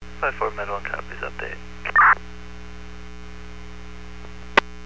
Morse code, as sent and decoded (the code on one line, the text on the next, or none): none